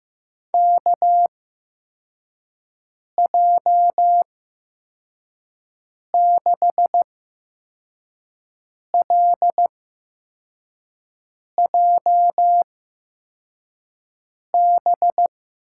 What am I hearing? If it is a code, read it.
Morse code, heard as a keyed tone, about 15 wpm, spaced as for 4 wpm, 700 Hz. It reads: KJ6LJB
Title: USA Call Signs - 6 Characters - 8x Character Spacing - 15wpm